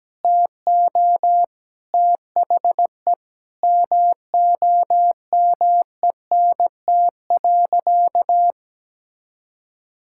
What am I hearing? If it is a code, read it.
Morse code, heard as a keyed tone, about 17 wpm, 700 Hz. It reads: TO THE MOMENT.
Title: Sentences from Top 1000 Words - Review Part 5 - 17wpm